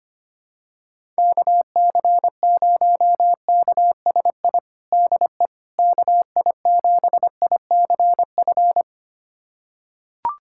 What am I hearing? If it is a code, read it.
Morse code, heard as a keyed tone, about 25 wpm, 700 Hz. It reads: KC0KSI DE KI7ICF
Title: USA Callsign DE Callsign - 6 Characters - 25wpm